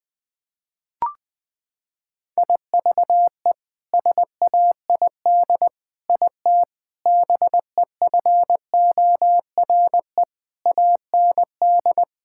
Morse code, heard as a keyed tone, about 21 wpm, 700 Hz. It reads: IVE SAID IT BEFORE AND
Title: Sets of 8 Words - 20wpm